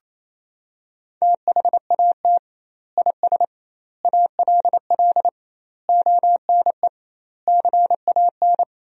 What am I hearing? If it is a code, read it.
Morse code, heard as a keyed tone, about 28 wpm, 700 Hz. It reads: THAT IS ALL ONE CAN
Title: 2x Word-Spacing - Sentences from Top 200 Words - 28wpm